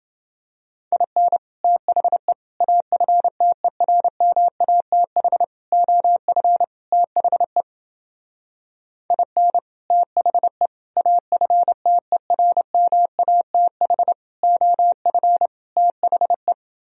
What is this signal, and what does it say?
Morse code, heard as a keyed tone, about 29 wpm, 700 Hz. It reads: IN THE AFTERMATH OF THE IN THE AFTERMATH OF THE